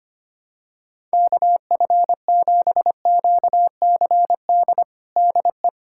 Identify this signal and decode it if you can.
Morse code, heard as a keyed tone, about 25 wpm, 700 Hz. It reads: KF7QCD DE
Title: USA Callsign DE Callsign 25wpm